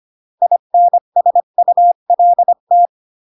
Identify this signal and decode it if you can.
Morse code, heard as a keyed tone, about 25 wpm, 700 Hz. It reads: INSULT